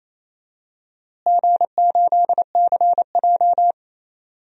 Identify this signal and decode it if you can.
Morse code, heard as a keyed tone, about 28 wpm, 700 Hz. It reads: G8CJ
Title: DXCC-100 Call Signs - 4 Characters - Repeated 2x - 28wpm